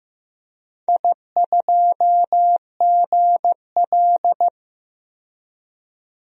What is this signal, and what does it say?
Morse code, heard as a keyed tone, about 15 wpm, 700 Hz. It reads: I2GL
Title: DXCC-100 Call Signs - 4 Characters - 15wpm